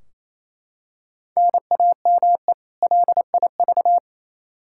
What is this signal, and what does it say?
Morse code, heard as a keyed tone, about 28 wpm, 700 Hz. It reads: NAME LIV
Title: Practice QSO - Exchange 1 - Beginning - 28wpm